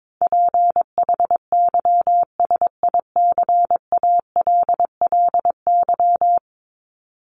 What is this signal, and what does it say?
Morse code, heard as a keyed tone, about 22 wpm, 700 Hz. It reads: PHYSICALLY
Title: Top 1501-1750 Words 22wpm